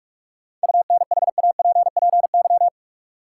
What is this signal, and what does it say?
Morse code, heard as a keyed tone, about 47 wpm, 700 Hz. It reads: UNHAPPY
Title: Seven Letter Words - 45wpm